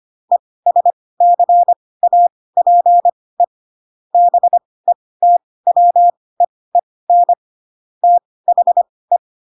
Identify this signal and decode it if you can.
Morse code, heard as a keyed tone, about 25 wpm, spaced as for 18 wpm, 700 Hz. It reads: ESCAPE BETWEEN THE
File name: Pinocchio ch3 12 25 ik1hge